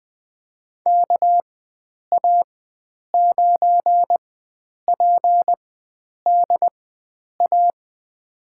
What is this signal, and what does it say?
Morse code, heard as a keyed tone, about 21 wpm, spaced as for 9 wpm, 700 Hz. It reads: KA9PDA
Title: USA Call Signs - 6 Characters - 4x Character Spacing - 20wpm